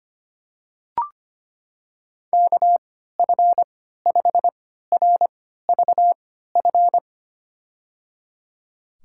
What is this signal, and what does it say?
Morse code, heard as a keyed tone, about 25 wpm, spaced as for 14 wpm, 700 Hz. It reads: KF5RVF
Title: USA Call Signs - 6 Characters - 3x Character Spacing - 25wpm